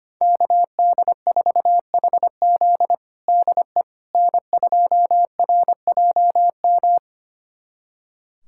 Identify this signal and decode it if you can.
Morse code, heard as a keyed tone, about 25 wpm, 700 Hz. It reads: KD4HZ DE N2RJM
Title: USA Callsign DE Callsign - 5 Characters - 25wpm